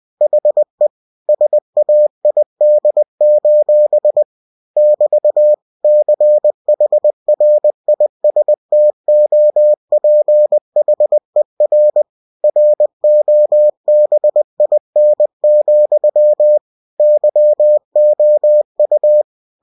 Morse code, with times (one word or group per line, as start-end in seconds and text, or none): 0.2-0.9 HE
1.3-4.2 SAID:
4.8-5.5 <BT>
5.8-12.0 CHRISTOPHER
12.4-16.6 ROBIN,
17.0-19.2 YOU